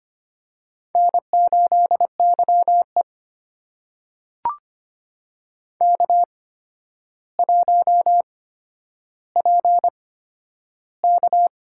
Morse code, heard as a keyed tone, about 25 wpm, spaced as for 11 wpm, 700 Hz. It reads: N8YE K1PK
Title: USA Call Signs - 4 Characters - 8x Character Spacing - 25wpm